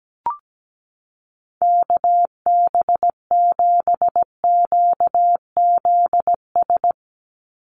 Morse code, heard as a keyed tone, about 17 wpm, 700 Hz. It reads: KB7QZS
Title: USA Call Signs 17wpm